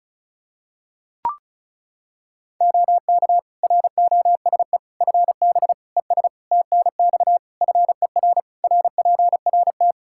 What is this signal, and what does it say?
Morse code, heard as a keyed tone, about 36 wpm, 700 Hz. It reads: OK ROSE FB ES TNX FER RPRT
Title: Practice QSO - Exchange 2 - Info - 35wpm